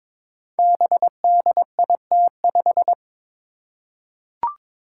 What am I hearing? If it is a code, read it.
Morse code, heard as a keyed tone, about 22 wpm, 700 Hz. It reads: BDIT5
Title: Five Challenging Letter-Number Combinations 22wpm